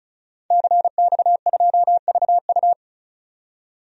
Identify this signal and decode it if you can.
Morse code, heard as a keyed tone, about 34 wpm, 700 Hz. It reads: CX2VU
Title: DXCC-100 Call Signs - 5 Characters - 35wpm